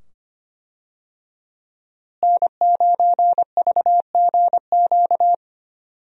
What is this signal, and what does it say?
Morse code, heard as a keyed tone, about 25 wpm, 700 Hz. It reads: N9VGQ
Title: USA Call Signs - 5 Characters - Repeated 2x - 25wpm